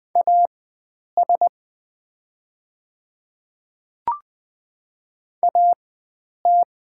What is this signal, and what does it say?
Morse code, heard as a keyed tone, about 20 wpm, spaced as for 9 wpm, 700 Hz. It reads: AS AT
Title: 4x Character Spacing - Top 100 Words - 20wpm